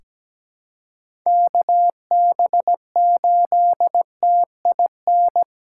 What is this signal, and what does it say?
Morse code, heard as a keyed tone, about 17 wpm, 700 Hz. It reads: KB8TIN
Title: USA Call Signs - 6 Characters - 2x Character Spacing - 17wpm